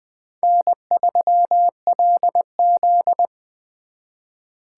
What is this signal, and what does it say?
Morse code, heard as a keyed tone, about 23 wpm, 700 Hz. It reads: N3LZ
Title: Speed Racing - USA Call Signs 20wpm